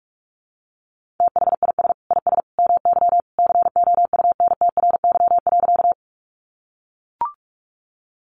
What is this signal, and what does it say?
Morse code, heard as a keyed tone, about 44 wpm, 700 Hz. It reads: THIS IS MY COUNTRY.
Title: Sentences from Top 300 Words 45wpm